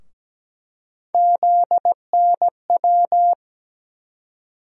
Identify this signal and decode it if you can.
Morse code, heard as a keyed tone, about 17 wpm, 700 Hz. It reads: ZNW
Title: USA Call Sign Postfixes - 3 Letters 17wpm